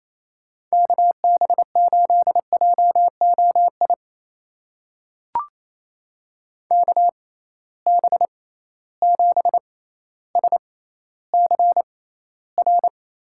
Morse code, heard as a keyed tone, about 28 wpm, spaced as for 14 wpm, 700 Hz. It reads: KB8JOI KB7SCR
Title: USA Call Signs - 6x Character Spacing - 28wpm